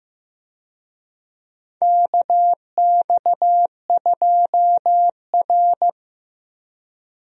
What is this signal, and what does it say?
Morse code, heard as a keyed tone, about 15 wpm, 700 Hz. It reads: KX2R
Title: USA Call Signs - 4 Characters - Repeated 3x - 15wpm